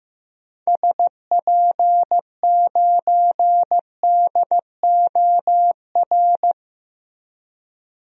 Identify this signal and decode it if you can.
Morse code, heard as a keyed tone, about 15 wpm, 700 Hz. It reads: SP9DOR
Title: DXCC-100 Call Signs - 6 Characters - Repeated 3x - 15wpm